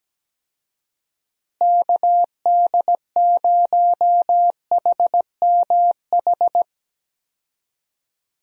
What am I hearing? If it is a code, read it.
Morse code, heard as a keyed tone, about 17 wpm, 700 Hz. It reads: KD0HMH
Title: USA Call Signs - 6 Characters - Repeated 2x - 17wpm